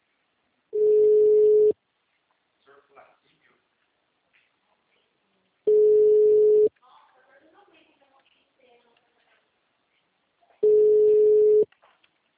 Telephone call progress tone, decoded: ringback tone

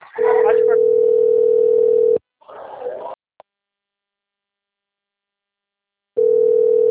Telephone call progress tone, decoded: ringback tone

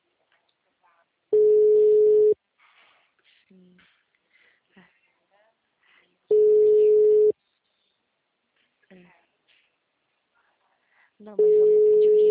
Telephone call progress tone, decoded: ringback tone

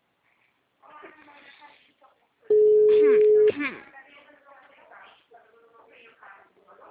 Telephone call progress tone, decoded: ringback tone

